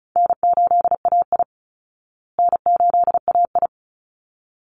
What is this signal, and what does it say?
Morse code, heard as a keyed tone, about 34 wpm, 700 Hz. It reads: N8AI N8AI
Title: Call Signs Repeated 3x - 35wpm